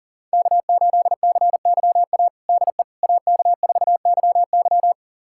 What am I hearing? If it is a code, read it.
Morse code, heard as a keyed tone, about 40 wpm, 700 Hz. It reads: K8CYA DE AK4YY
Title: USA Callsign DE Callsign - 5 Characters - 40wpm